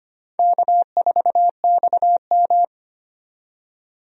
Morse code, heard as a keyed tone, about 25 wpm, 700 Hz. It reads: K4XM
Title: USA Call Signs - 4 Characters - 25wpm